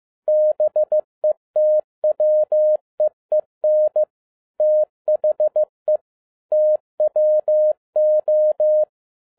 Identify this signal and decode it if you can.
Morse code, heard as a keyed tone, about 15 wpm, 610 Hz. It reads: BETWEEN THE TWO
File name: 2011-10-mp3 / n07-A DOT 15wpm.mp3